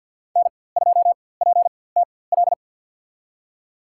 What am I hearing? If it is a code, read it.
Morse code, heard as a keyed tone, about 51 wpm, spaced as for 23 wpm, 700 Hz. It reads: N2PTL
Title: USA Call Signs - 5 Characters - 4x Character Spacing - 50wpm